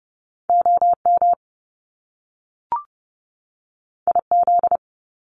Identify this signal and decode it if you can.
Morse code, heard as a keyed tone, about 30 wpm, 700 Hz. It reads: OM IZ
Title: Top Two Letter Combinations in Words 30wpm